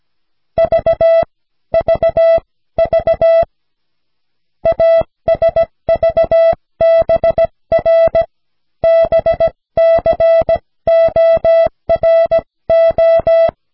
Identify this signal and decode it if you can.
Morse code, heard as a keyed tone, about 17 wpm, spaced as for 15 wpm, 660 Hz. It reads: V VV ASVBR BCORO
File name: cw / snelheid 4